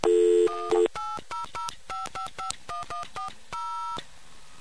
DTMF, heard as telephone touch tones, tones of about 140 ms. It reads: 18005551140